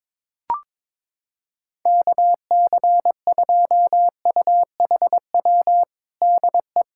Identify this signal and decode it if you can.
Morse code, heard as a keyed tone, about 22 wpm, 700 Hz. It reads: KC2UHW DE
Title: Practice QSO - Exchange 4 - Ending - 22wpm